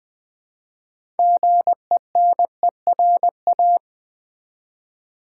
Morse code, heard as a keyed tone, about 20 wpm, 700 Hz. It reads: GENERA
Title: Build A Word - 3 Letters At A Time - 8-Letter Words - 20wpm